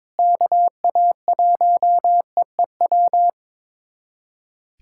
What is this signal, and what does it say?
Morse code, heard as a keyed tone, about 22 wpm, 700 Hz. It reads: KA1EEW
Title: USA Callsign DE Callsign 22wpm